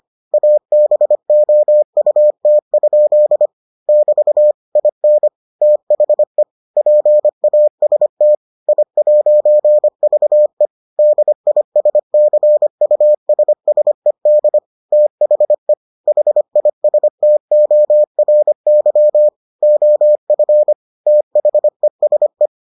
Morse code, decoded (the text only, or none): ABOUT? <BT> IN THE PAST I'VE DISCUSSED THE HISTORY OF THESE